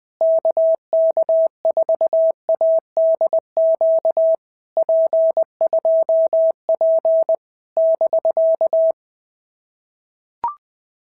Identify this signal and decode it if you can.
Morse code, heard as a keyed tone, about 19 wpm, 650 Hz. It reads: KK4ADQ P2P <BK>
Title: POTA QSO Practice - 20wpm